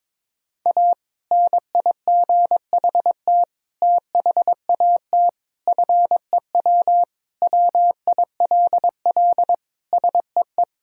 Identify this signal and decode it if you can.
Morse code, heard as a keyed tone, about 22 wpm, 700 Hz. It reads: A NIGHT THAT FEW WILL SEE
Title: Sentences from Top 1000 Words - Review Part 1 - 22wpm